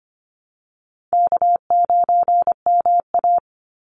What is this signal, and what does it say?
Morse code, heard as a keyed tone, about 25 wpm, 700 Hz. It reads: K9MA
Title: Call Signs Repeated 2x - 25wpm